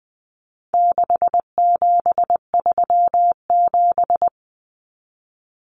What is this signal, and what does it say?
Morse code, heard as a keyed tone, about 20 wpm, 700 Hz. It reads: 6737
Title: Common Cut Numbers - 4-Digit 20wpm